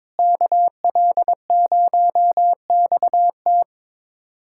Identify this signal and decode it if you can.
Morse code, heard as a keyed tone, about 22 wpm, 700 Hz. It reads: KL0XT
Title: USA Callsign DE Callsign - 5 Characters - 22wpm